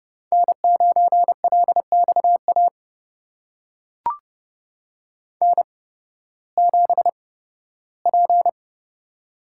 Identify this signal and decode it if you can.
Morse code, heard as a keyed tone, about 30 wpm, spaced as for 16 wpm, 700 Hz. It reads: N9LXA N7P
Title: USA Call Signs - 5 Characters - 8x Character Spacing - 30wpm